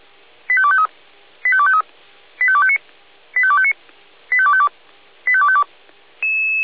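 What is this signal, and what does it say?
Five-tone selective call (ZVEI), heard as quick runs of five tones, tones of about 70 ms. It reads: 85253 85253 85259 85259 85252 85252
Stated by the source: ZVEI five-tone call 85280 received by SDR.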